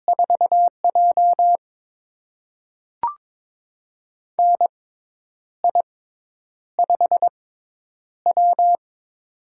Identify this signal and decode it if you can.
Morse code, heard as a keyed tone, about 23 wpm, spaced as for 9 wpm, 700 Hz. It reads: 4J NI5W
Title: USA Call Signs - 6x Character Spacing - 22wpm